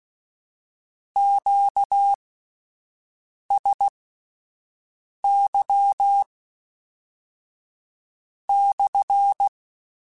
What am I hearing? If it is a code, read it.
Morse code, heard as a keyed tone, about 16 wpm, spaced as for 5 wpm, 800 Hz. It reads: QSY/
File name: Q Calls / Q Calls-mp3s / Q Calls SPSL1 @10